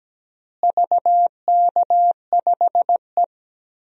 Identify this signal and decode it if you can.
Morse code, heard as a keyed tone, about 17 wpm, 700 Hz. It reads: VK5E
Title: DXCC-100 Call Signs - 4 Characters - Repeated 2x - 17wpm